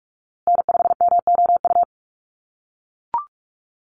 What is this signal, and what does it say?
Morse code, heard as a keyed tone, about 44 wpm, 700 Hz. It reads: N5MOU
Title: USA Call Signs Repeated 3x 45wpm